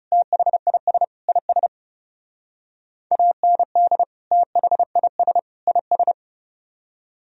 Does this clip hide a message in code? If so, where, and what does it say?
Yes: Morse code from 0.1 to 6.1 s, heard as a keyed tone, about 39 wpm, 700 Hz. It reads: THIS IS AND THIS IS